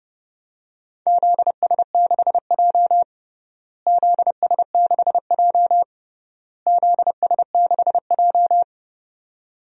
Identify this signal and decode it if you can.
Morse code, heard as a keyed tone, about 30 wpm, 700 Hz. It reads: ZS6J ZS6J ZS6J
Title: DXCC-100 Call Signs - 4 Characters - Repeated 3x - 30wpm